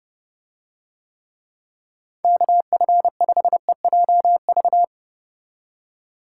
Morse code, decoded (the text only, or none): KF5EJV